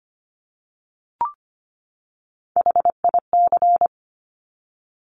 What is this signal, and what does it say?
Morse code, heard as a keyed tone, about 25 wpm, 700 Hz. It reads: HIC
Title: Top Three Letter Combinations in Words 25wpm